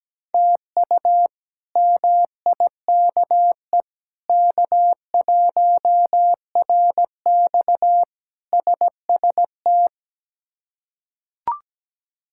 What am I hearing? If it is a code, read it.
Morse code, heard as a keyed tone, about 17 wpm, 700 Hz. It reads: TU MIKE K1RX SST
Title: K1USN SST QSO Practice - 17wpm